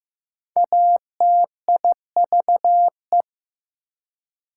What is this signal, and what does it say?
Morse code, heard as a keyed tone, about 15 wpm, 700 Hz. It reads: ATIVE
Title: Prefixes and Suffixes 15wpm